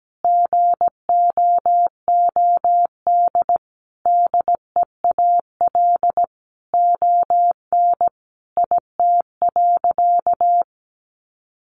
Morse code, heard as a keyed tone, about 17 wpm, 700 Hz. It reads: GOOD DEAL ON IT.
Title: Sentences from Top 1300 Words 17wpm